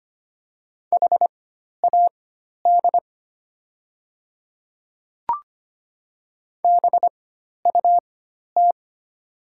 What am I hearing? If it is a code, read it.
Morse code, heard as a keyed tone, about 25 wpm, spaced as for 12 wpm, 690 Hz. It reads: HAD BUT
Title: 4x Character Spacing - Top 300 Words - 25wpm